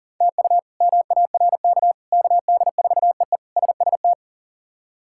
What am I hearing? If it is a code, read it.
Morse code, heard as a keyed tone, about 37 wpm, 700 Hz. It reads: TU MARK KD4EE SST
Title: K1USN SST QSO Practice - 40wpm